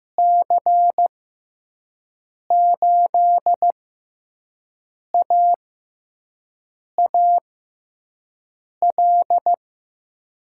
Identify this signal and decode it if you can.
Morse code, heard as a keyed tone, about 15 wpm, spaced as for 5 wpm, 700 Hz. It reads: C8AAL
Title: USA Call Signs - 6 Characters - 6x Character Spacing - 15wpm